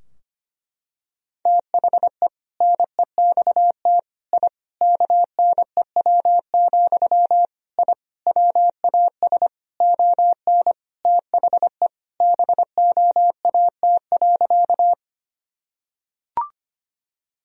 Morse code, heard as a keyed tone, about 25 wpm, 700 Hz. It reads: THE NEXT I KNEW, I WAS ON THE BOAT.
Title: Sentences from Top 1000 Words - Review Part 3 - 25wpm